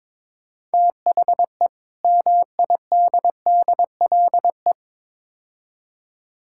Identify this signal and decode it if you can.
Morse code, heard as a keyed tone, about 21 wpm, 700 Hz. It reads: THE MIDDLE
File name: practice / 22wpm / Build A Phrase - 5-Word Phrases - 22wpm